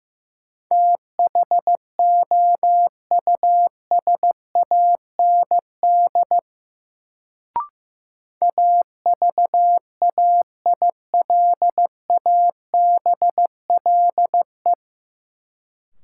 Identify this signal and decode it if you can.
Morse code, heard as a keyed tone, about 15 wpm, 700 Hz. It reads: THOUSAND AVAILABLE
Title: Sentences from Top 600 Words - New Word Drill - 15wpm